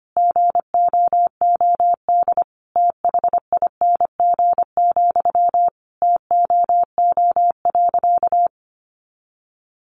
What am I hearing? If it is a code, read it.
Morse code, heard as a keyed tone, about 25 wpm, 700 Hz. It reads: GOOD THING, TOO.
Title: Sentences from Top 200 Words 25wpm